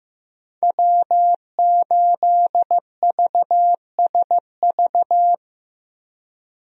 Morse code, heard as a keyed tone, about 15 wpm, 700 Hz. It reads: W8VSV
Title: USA Call Signs - 5 Characters - Repeated 2x - 15wpm